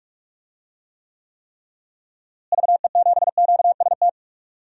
Morse code, heard as a keyed tone, about 41 wpm, 700 Hz. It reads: VE7QST
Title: DXCC-100 Call Signs - 6 Characters - Repeated 3x - 45wpm